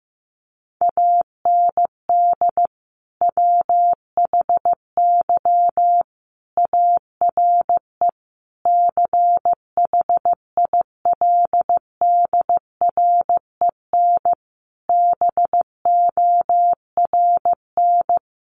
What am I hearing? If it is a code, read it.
Morse code, heard as a keyed tone, about 14 wpm, 700 Hz. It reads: AND WHY ARE CHILDREN BORN
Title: Sentences from Top 1000 Words 15wpm